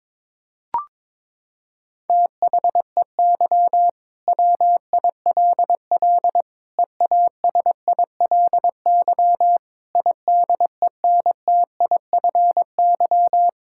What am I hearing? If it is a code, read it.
Morse code, heard as a keyed tone, about 21 wpm, 700 Hz. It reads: THEY WILL EASILY IDENTIFY
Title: Sentences from Top 1000 Words - Review Part 6 - 22wpm